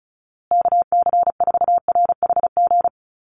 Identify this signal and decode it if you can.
Morse code, heard as a keyed tone, about 34 wpm, 700 Hz. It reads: KC4RHG
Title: USA Call Signs Repeated 2x 35wpm